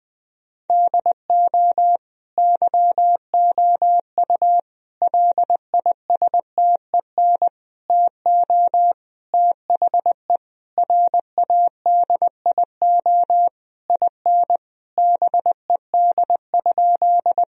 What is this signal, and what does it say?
Morse code, heard as a keyed tone, about 21 wpm, 700 Hz. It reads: DO YOU LISTEN TO THE RADIO IN BED?
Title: Sentences from Top 1000 Words - Review Part 3 - 20wpm